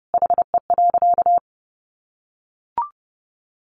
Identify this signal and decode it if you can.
Morse code, heard as a keyed tone, about 30 wpm, 700 Hz. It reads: HE.
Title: Sentences from Top 100 Words 30wpm